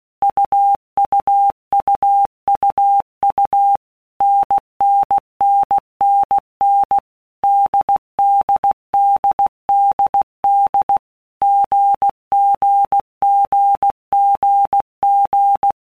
Morse code, heard as a keyed tone, about 16 wpm, 800 Hz. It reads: UUUUU NNNNN DDDDD GGGGG
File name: All Groups 1-minute Blocks / Groups@15 / AWUNDG